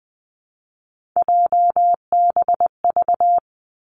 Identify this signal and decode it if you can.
Morse code, heard as a keyed tone, about 20 wpm, 700 Hz. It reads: JBV
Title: Three Challenging Letter-Number Combinations 20wpm